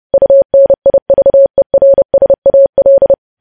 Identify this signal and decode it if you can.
Morse code, heard as a keyed tone, about 30 wpm, 550 Hz. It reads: UNIVERSAL
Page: news headlines in morse code at 30 wpm